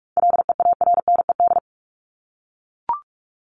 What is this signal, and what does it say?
Morse code, heard as a keyed tone, about 44 wpm, 700 Hz. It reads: LEARNED
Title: ED Suffix - Top 100 Words with Base Words 45wpm